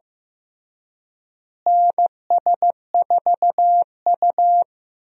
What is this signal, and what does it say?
Morse code, heard as a keyed tone, about 15 wpm, 700 Hz. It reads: NS4U
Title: USA Call Signs - 4 Characters - Repeated 3x - 15wpm